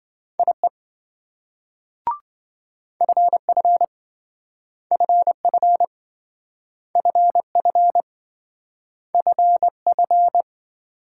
Morse code, heard as a keyed tone, about 26 wpm, 700 Hz. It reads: IE FF FF FF FF